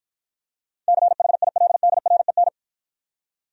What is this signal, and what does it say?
Morse code, heard as a keyed tone, about 53 wpm, 700 Hz. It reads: CHILDREN